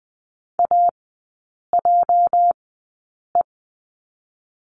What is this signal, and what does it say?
Morse code, heard as a keyed tone, about 20 wpm, spaced as for 8 wpm, 700 Hz. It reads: AJE
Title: 218 Lesson 35e - Number 0 - Three Characters 20wpm